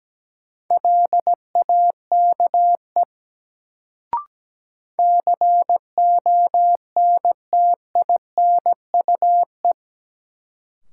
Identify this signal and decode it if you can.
Morse code, heard as a keyed tone, about 17 wpm, 700 Hz. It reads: LAKE CONTINUE
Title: Sentences from Top 1100 Words - New Word Drill Alt - 17wpm